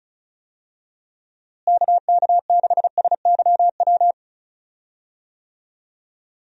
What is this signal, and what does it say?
Morse code, heard as a keyed tone, about 36 wpm, 700 Hz. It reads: KK6SYW
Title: USA Call Signs - 6 Characters - Repeated 3x - 35wpm